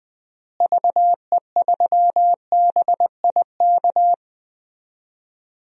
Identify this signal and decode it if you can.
Morse code, heard as a keyed tone, about 20 wpm, 700 Hz. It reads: VE3BIK
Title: DXCC-100 Call Signs - 6 Characters - Repeated 3x - 20wpm